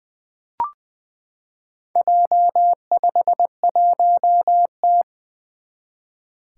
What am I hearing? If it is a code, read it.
Morse code, heard as a keyed tone, about 20 wpm, 700 Hz. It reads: J51T
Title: Four Challenging Letter-Number Combinations 20wpm